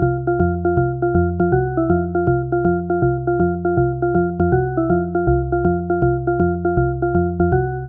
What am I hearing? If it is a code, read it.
There is no code.